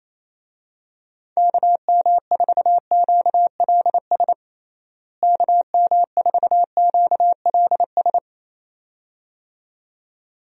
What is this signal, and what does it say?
Morse code, heard as a keyed tone, about 28 wpm, 700 Hz. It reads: KM4QLS KM4QLS
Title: USA Call Signs - 6 Characters - Repeated 2x - 28wpm